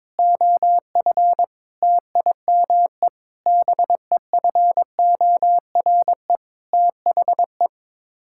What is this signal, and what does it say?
Morse code, heard as a keyed tone, about 22 wpm, 700 Hz. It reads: OF TIME BEFORE THE
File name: Sets of 9 Words - 22wpm